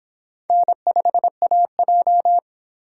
Morse code, heard as a keyed tone, about 24 wpm, 700 Hz. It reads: N5AJ